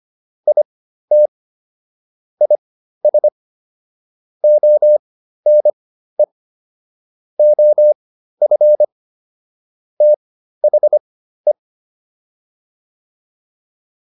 Morse code, heard as a keyed tone, about 25 wpm, spaced as for 13 wpm, 600 Hz. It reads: IT IS ONE OF THE